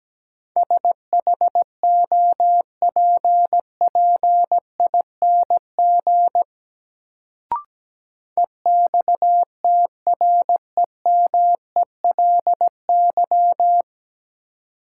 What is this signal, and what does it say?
Morse code, heard as a keyed tone, about 17 wpm, 700 Hz. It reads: SHOPPING EXTREMELY
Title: Sentences from Top 1000 Words - Review Part 7 - New Word Drill - 17wpm